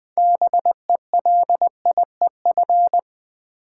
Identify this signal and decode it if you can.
Morse code, heard as a keyed tone, about 20 wpm, 700 Hz. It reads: BELIEF
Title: Build A Word - 2 Letters At A Time - 6-Letter Words - 20wpm